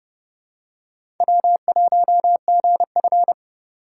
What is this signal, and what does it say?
Morse code, heard as a keyed tone, about 30 wpm, 700 Hz. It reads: W1GF
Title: USA Call Signs - 4x Character Spacing - 30wpm